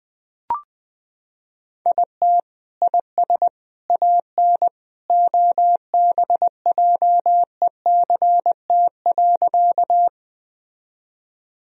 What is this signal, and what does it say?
Morse code, heard as a keyed tone, about 20 wpm, 700 Hz. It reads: IT IS AN OBJECT.
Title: Sentences from Top 1000 Words - Review Part 3 - 20wpm